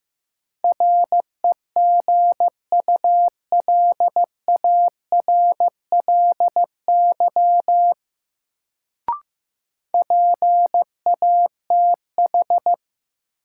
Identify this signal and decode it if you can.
Morse code, heard as a keyed tone, about 16 wpm, 700 Hz. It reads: REGULARLY PATH
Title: Sentences from Top 1400 Words - New Word Drill - 15wpm